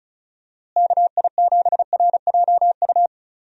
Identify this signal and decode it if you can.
Morse code, heard as a keyed tone, about 36 wpm, 700 Hz. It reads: KI7RJU
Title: USA Call Signs - 6 Characters - Repeated 3x - 35wpm